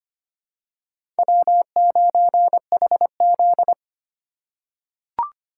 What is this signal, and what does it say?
Morse code, heard as a keyed tone, about 25 wpm, 700 Hz. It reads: W9HZ
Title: USA Call Signs - 4 Characters - 4x Character Spacing - 25wpm